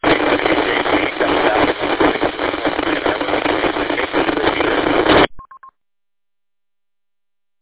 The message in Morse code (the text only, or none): none